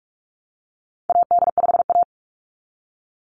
Morse code, heard as a keyed tone, about 44 wpm, 700 Hz. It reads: AD5A
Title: NA SOTA Activator Callsigns 45wpm